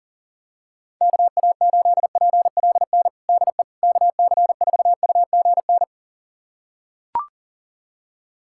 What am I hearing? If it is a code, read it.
Morse code, heard as a keyed tone, about 40 wpm, 700 Hz. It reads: KA8PLN DE KC4UGN
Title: USA Callsign DE Callsign 40wpm